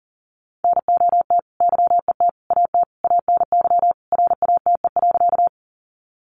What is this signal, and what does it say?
Morse code, heard as a keyed tone, about 40 wpm, 700 Hz. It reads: NOT YET AT ANY RATE.
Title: Sentences from Top 700 Words 40wpm